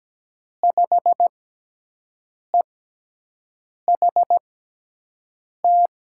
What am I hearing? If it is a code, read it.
Morse code, heard as a keyed tone, about 17 wpm, spaced as for 6 wpm, 700 Hz. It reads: 5EHT
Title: USA Call Signs - 5 Characters - 6x Character Spacing - 17wpm